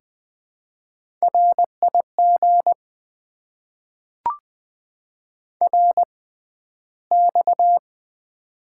Morse code, heard as a keyed tone, about 20 wpm, spaced as for 12 wpm, 700 Hz. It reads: RIG RX
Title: Top 90 QSO Elements - 6x Character Spacing - 20wpm